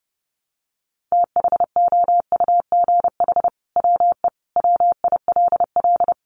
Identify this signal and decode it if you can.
Morse code, heard as a keyed tone, about 30 wpm, 700 Hz. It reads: THOUGH WE WILL
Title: Sentences from Top 1000 Words 30wpm